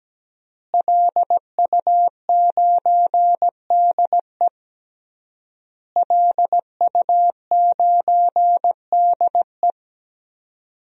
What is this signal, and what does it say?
Morse code, heard as a keyed tone, about 17 wpm, 700 Hz. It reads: LU9DE LU9DE
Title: DXCC-100 Call Signs - 5 Characters - Repeated 3x - 17wpm